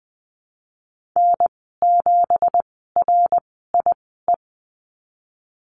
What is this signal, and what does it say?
Morse code, heard as a keyed tone, about 20 wpm, spaced as for 15 wpm, 700 Hz. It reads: N7RIE
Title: 167 Lesson 27g - Number 7 - Call Signs 20wpm